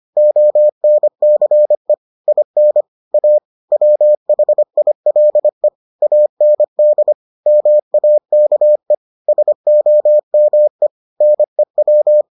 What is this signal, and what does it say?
Morse code, heard as a keyed tone, about 25 wpm, 600 Hz. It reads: ONCE IN A WHILE AND MAKE SOME NEW